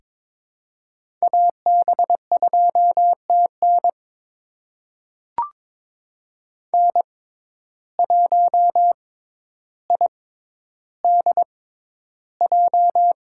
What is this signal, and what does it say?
Morse code, heard as a keyed tone, about 22 wpm, spaced as for 11 wpm, 700 Hz. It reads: AB2TN N1IDJ
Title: USA Call Signs - 5 Characters - 6x Character Spacing - 22wpm